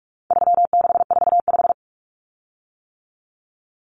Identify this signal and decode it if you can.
Morse code, heard as a keyed tone, about 44 wpm, 700 Hz. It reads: 3645
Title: Common Cut Numbers - 4-Digit 45wpm